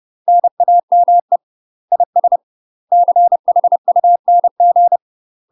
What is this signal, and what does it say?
Morse code, heard as a keyed tone, about 30 wpm, 700 Hz. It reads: NAME IS CHUNG
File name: UK MASTER QSOs TRAINING / 30WPM / UK MASTER QSOs 0047